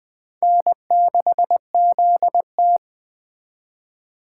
Morse code, heard as a keyed tone, about 24 wpm, 700 Hz. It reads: N6ZT
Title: Speed Racing - USA 1x2 Call Signs 20wpm